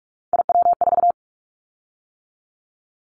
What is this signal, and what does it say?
Morse code, heard as a keyed tone, about 46 wpm, 700 Hz. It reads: IW4
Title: Callsign Prefixes for Top 25 DXCC - 45wpm